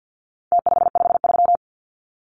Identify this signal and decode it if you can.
Morse code, heard as a keyed tone, about 50 wpm, 700 Hz. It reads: T553
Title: Numbers - Contest Style - 4-Digit 50wpm